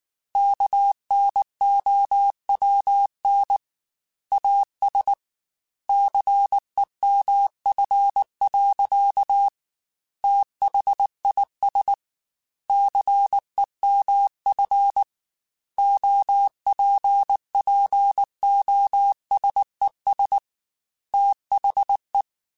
Morse code, heard as a keyed tone, about 19 wpm, 800 Hz. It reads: KNOWN AS CEMF. THIS CEMF OPPOSES THE